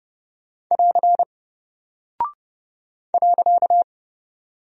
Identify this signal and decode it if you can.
Morse code, heard as a keyed tone, about 30 wpm, 700 Hz. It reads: <AR> .
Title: Single Character (Letter:Number:Punctuation:Prosign) Alt 30wpm